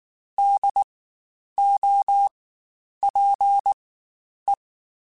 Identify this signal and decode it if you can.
Morse code, heard as a keyed tone, about 19 wpm, spaced as for 9 wpm, 800 Hz. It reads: DOPE
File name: Spaced Alphas / Spaced Alphas mp3s / Ds sp @18